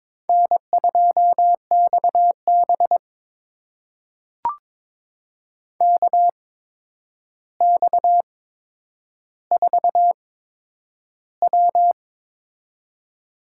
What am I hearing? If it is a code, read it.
Morse code, heard as a keyed tone, about 22 wpm, spaced as for 9 wpm, 700 Hz. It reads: N2XB KX4W
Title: USA Call Signs - 4 Characters - 8x Character Spacing - 22wpm